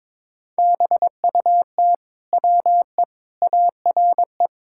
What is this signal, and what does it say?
Morse code, heard as a keyed tone, about 22 wpm, 700 Hz. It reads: BUT WE ARE
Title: Sentences from Top 1600 Words 22wpm